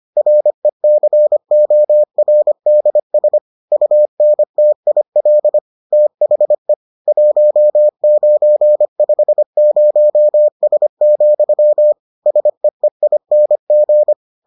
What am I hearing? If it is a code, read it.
Morse code, heard as a keyed tone, about 25 wpm, 600 Hz. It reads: RECORDS UNTIL THE 1950S, SEEING